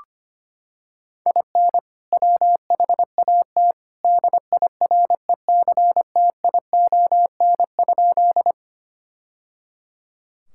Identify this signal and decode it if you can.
Morse code, heard as a keyed tone, about 25 wpm, 700 Hz. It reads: IN WHAT DIRECTION?
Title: Sentences from Top 1000 Words - Review Part 5 - 25wpm